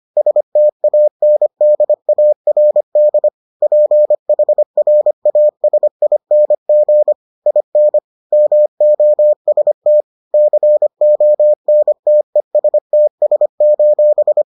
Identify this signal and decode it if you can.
Morse code, heard as a keyed tone, about 25 wpm, 600 Hz. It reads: STANDARD PHRASING IN MOST CONTESTS: